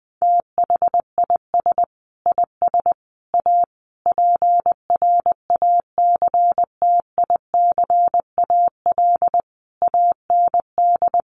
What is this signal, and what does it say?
Morse code, heard as a keyed tone, about 20 wpm, 700 Hz. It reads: THIS IS A PRACTICAL AND
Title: Sentences from Top 1400 Words 20wpm